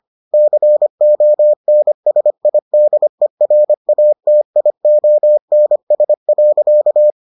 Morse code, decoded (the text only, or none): CONSIDERATIONS.